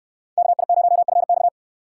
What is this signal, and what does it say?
Morse code, heard as a keyed tone, about 64 wpm, 700 Hz. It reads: KI0FX